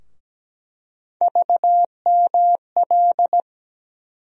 Morse code, heard as a keyed tone, about 17 wpm, 700 Hz. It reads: VML
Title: USA Call Sign Postfixes - 3 Letters 17wpm